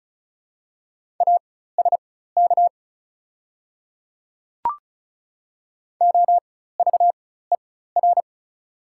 Morse code, heard as a keyed tone, about 36 wpm, spaced as for 16 wpm, 690 Hz. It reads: ASK OVER
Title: 4x Character Spacing - Top 300 Words - 35wpm